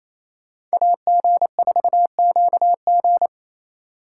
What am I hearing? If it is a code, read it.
Morse code, heard as a keyed tone, about 34 wpm, 700 Hz. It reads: AG4QG
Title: Speed Racing - USA 2x2 Call Signs 28wpm